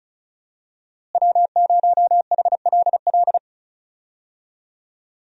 Morse code, heard as a keyed tone, about 36 wpm, 700 Hz. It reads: W0HLL
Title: USA Call Signs - 5 Characters - Repeated 2x - 35wpm